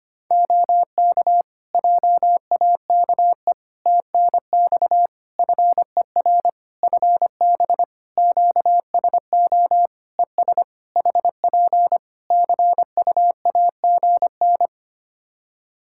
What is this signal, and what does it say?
Morse code, heard as a keyed tone, about 25 wpm, 700 Hz. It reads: OK JAKE TNX FER FB QSO ES HP CUAGN
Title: Practice QSO - Exchange 4 - Ending - 25wpm